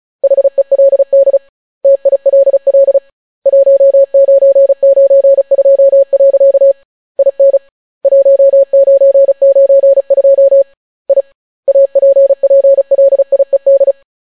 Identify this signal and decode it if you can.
Morse code, heard as a keyed tone, about 36 wpm, 550 Hz. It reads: HELD TILL 1992. IN 1992 I APPLIED